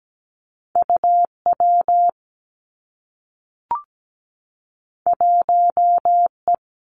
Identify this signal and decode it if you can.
Morse code, heard as a keyed tone, about 17 wpm, 700 Hz. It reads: UW 1E
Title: Two Challenging Letter-Number Combinations 17wpm